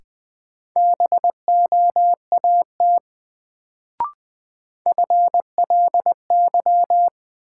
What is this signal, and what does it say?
Morse code, heard as a keyed tone, about 20 wpm, 700 Hz. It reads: BOAT FLY
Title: Sentences from Top 500 Words - New Word Drill - 20wpm